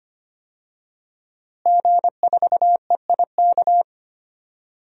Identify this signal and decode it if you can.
Morse code, heard as a keyed tone, about 25 wpm, 700 Hz. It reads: G4EIK
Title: DXCC-100 Call Signs - 5 Characters - Repeated 2x - 25wpm